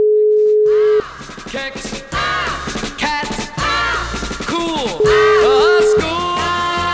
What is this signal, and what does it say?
Ringback tone (Europe, 425 Hz), with speech.